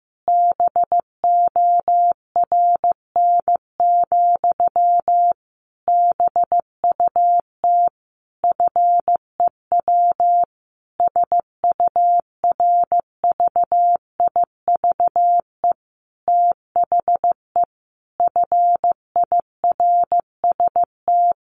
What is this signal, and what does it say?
Morse code, heard as a keyed tone, about 15 wpm, 700 Hz. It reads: BORN, BUT FEW SURVIVE THE FIRST
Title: Sentences from Top 1300 Words 15wpm